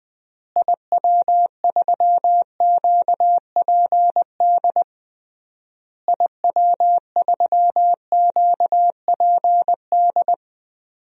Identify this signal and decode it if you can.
Morse code, heard as a keyed tone, about 20 wpm, 700 Hz. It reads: IW3QPD IW3QPD
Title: DXCC-100 Call Signs - 6 Characters - Repeated 3x - 20wpm